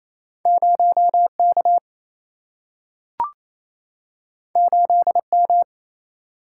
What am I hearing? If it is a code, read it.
Morse code, heard as a keyed tone, about 28 wpm, 700 Hz. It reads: 0K 8M
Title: Single-Letter Single-Number 28wpm